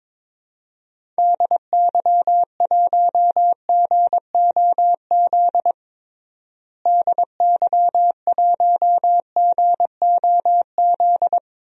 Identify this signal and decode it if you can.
Morse code, heard as a keyed tone, about 22 wpm, 700 Hz. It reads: DY1GOZ DY1GOZ
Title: DXCC-100 Call Signs - 6 Characters - Repeated 2x - 22wpm